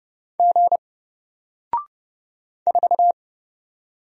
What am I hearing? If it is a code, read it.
Morse code, heard as a keyed tone, about 31 wpm, 700 Hz. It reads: G 4
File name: Single-Letter-Number-30wpm